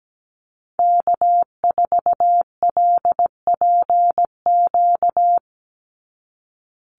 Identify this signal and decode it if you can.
Morse code, heard as a keyed tone, about 17 wpm, 700 Hz. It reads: K4LPQ
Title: NA SOTA Activator Callsigns 17wpm